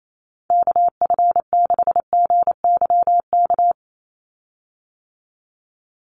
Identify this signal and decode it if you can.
Morse code, heard as a keyed tone, about 28 wpm, 700 Hz. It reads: KF6GYK
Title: USA Call Signs 28wpm